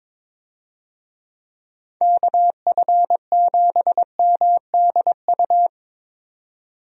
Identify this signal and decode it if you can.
Morse code, heard as a keyed tone, about 21 wpm, 700 Hz. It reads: KF7MDU
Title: USA Call Signs - 6 Characters - 22wpm